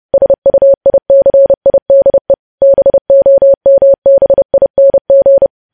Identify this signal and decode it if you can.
Morse code, heard as a keyed tone, about 30 wpm, 550 Hz. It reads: SUICIDE BOMBING
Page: news headlines in morse code at 30 wpm